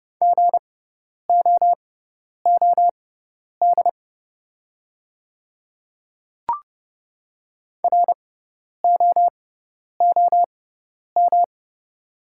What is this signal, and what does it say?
Morse code, heard as a keyed tone, about 30 wpm, spaced as for 10 wpm, 700 Hz. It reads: GOOD ROOM
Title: 6x Character Spacing - Top 300 Words - 30wpm